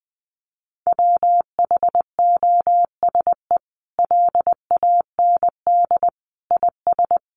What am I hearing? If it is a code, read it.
Morse code, heard as a keyed tone, about 20 wpm, 700 Hz. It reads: WHOSE LAND IS
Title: Sentences from Top 1200 Words 20wpm